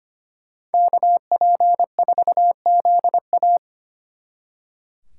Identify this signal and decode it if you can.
Morse code, heard as a keyed tone, about 33 wpm, 700 Hz. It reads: KP4ZA